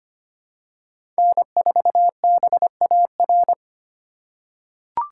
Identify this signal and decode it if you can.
Morse code, heard as a keyed tone, about 25 wpm, 700 Hz. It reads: N4BAR
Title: USA Call Signs - 5 Characters - 4x Character Spacing - 25wpm